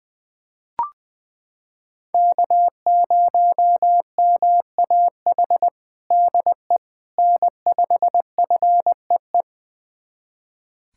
Morse code, heard as a keyed tone, about 21 wpm, 700 Hz. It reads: K0MAH DE N5FEE
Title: USA Callsign DE Callsign 20wpm